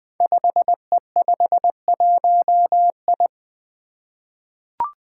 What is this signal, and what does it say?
Morse code, heard as a keyed tone, about 21 wpm, 700 Hz. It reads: 5E51I
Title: Five Challenging Letter-Number Combinations 20wpm